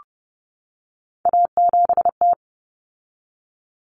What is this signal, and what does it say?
Morse code, heard as a keyed tone, about 30 wpm, 700 Hz. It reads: A7T
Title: Common Cut Numbers - 3-Digit 30wpm